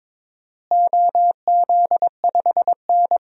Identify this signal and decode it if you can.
Morse code, heard as a keyed tone, about 22 wpm, 700 Hz. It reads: OZ5N
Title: DXCC-100 Call Signs - 4 Characters - Repeated 3x - 22wpm